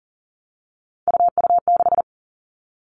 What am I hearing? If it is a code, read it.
Morse code, heard as a keyed tone, about 42 wpm, 690 Hz. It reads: UU6
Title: Three Challenging Letter-Number Combinations 40wpm